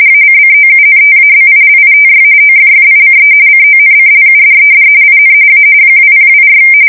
RTTY, about 45.45 baud. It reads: ACCOUNTABILITY OFFICE CONCLUDED IN A 2024
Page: 170 Shift 2125/2295